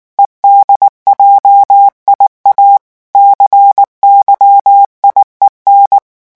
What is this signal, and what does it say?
Morse code, heard as a keyed tone, about 19 wpm, 800 Hz. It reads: EDJIA CYIEN
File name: morse mp3 / revision7-vitesse 18 mots